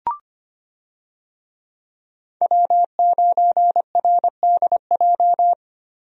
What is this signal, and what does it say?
Morse code, heard as a keyed tone, about 25 wpm, 700 Hz. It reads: W9RDJ
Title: USA Call Signs - 5 Characters - Repeated 2x - 25wpm